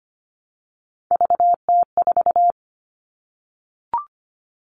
Morse code, heard as a keyed tone, about 25 wpm, 700 Hz. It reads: VT4
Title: All Cut Numbers - 3-Digit 25wpm